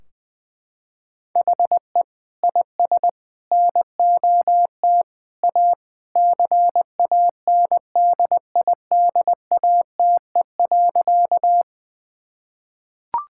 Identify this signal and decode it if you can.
Morse code, heard as a keyed tone, about 21 wpm, 700 Hz. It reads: HE IS NOT A CANDIDATE.
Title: Sentences from Top 1600 Words 20wpm